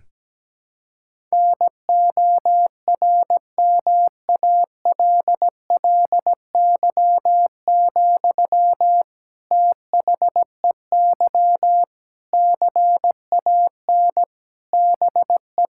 Morse code, heard as a keyed tone, about 17 wpm, 700 Hz. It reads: NORMALLY, THEY CAN BE
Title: Sentences from Top 1000 Words - Review Part 6 - 17wpm